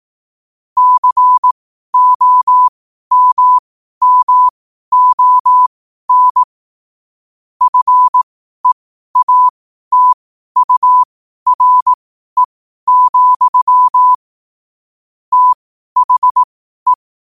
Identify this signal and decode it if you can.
Morse code, heard as a keyed tone, about 18 wpm, spaced as for 12 wpm, 1000 Hz. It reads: COMMON FEATURE, THE
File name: qotdfiles / 12wpm